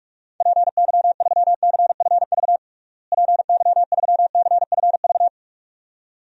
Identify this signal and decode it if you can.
Morse code, heard as a keyed tone, about 46 wpm, 700 Hz. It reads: PY3CFV PY3CFV
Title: DXCC-100 Call Signs - 6 Characters - Repeated 2x - 45wpm